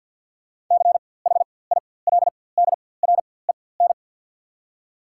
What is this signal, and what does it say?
Morse code, heard as a keyed tone, about 53 wpm, spaced as for 24 wpm, 690 Hz. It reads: CHILDREN
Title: 4x Character Spacing - Top 500 Words - 50wpm